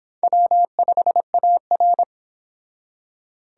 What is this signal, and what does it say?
Morse code, heard as a keyed tone, about 26 wpm, 700 Hz. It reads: W5AR